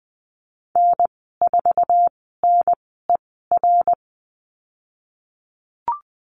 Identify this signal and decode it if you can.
Morse code, heard as a keyed tone, about 20 wpm, spaced as for 15 wpm, 700 Hz. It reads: N4NER
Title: 088 Lesson 15g - Number 5 - Call Signs 20wpm